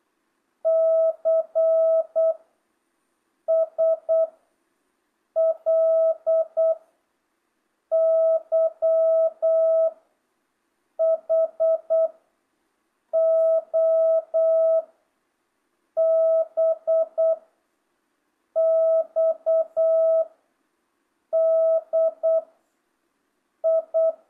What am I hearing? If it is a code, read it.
Morse code, heard as a keyed tone, about 8 wpm, 660 Hz. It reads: C S L Y H O B X D I
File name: jh8cbh / 2025cw